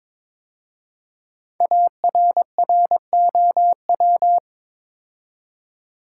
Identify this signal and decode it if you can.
Morse code, heard as a keyed tone, about 23 wpm, 700 Hz. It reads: ARROW
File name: 5 Letter Words - First 3 Letters Given - 22wpm